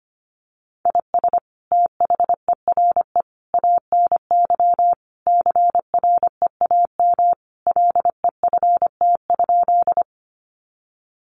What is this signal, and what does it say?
Morse code, heard as a keyed tone, about 25 wpm, 700 Hz. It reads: IS THERE ANY CREAM LEFT?
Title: Sentences from Top 1500 Words 25wpm